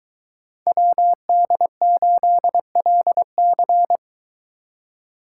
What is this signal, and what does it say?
Morse code, heard as a keyed tone, about 23 wpm, 700 Hz. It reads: WD8LC